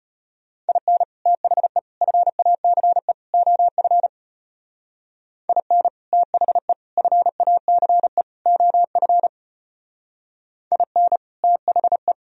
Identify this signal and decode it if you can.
Morse code, heard as a keyed tone, about 30 wpm, 700 Hz. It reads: IN THE FACE OF IN THE FACE OF IN THE
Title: Speed-Racing - Sets of 4 Words 25wpm